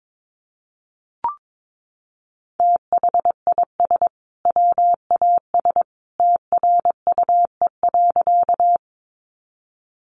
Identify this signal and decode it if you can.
Morse code, heard as a keyed tone, about 22 wpm, 700 Hz. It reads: THIS WAS TRUE.
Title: Sentences from Top 400 Words 22wpm